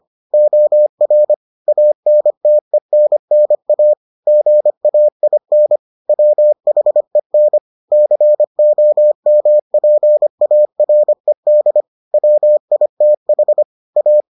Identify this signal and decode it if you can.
Morse code, heard as a keyed tone, about 25 wpm, 600 Hz. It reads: OR ANTENNA GAIN WHEN COMPARED WITH A